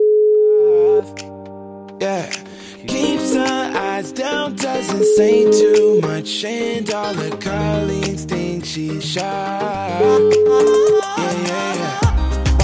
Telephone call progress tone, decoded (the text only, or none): ringback tone